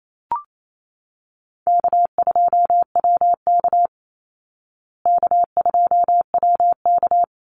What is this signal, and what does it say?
Morse code, heard as a keyed tone, about 28 wpm, 700 Hz. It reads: K2WK K2WK
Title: Easy Call Signs Repeated 2x - 28wpm